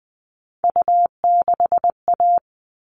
Morse code, heard as a keyed tone, about 20 wpm, 700 Hz. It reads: U6A
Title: All Cut Numbers - 3-Digit 20wpm